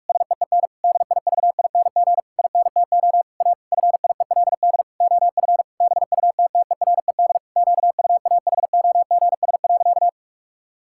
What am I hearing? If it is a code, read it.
Morse code, heard as a keyed tone, about 46 wpm, 700 Hz. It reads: SEEN DIVING INTO A FIELD OF BUTTERED QUAHOGS.
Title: Pangram Challenge v2 - 45wpm